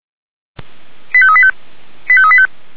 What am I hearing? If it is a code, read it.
Five-tone selective call (ZVEI), heard as quick runs of five tones, tones of about 70 ms. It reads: 85385 85385